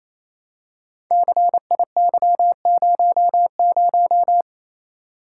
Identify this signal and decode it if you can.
Morse code, heard as a keyed tone, about 27 wpm, 700 Hz. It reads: CIY00
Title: Code Groups - 5 Letter:Numbers - 28wpm